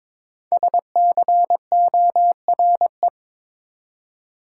Morse code, heard as a keyed tone, about 22 wpm, 700 Hz. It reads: SCORE
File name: 5 Letter Words - First 3 Letters Given - 22wpm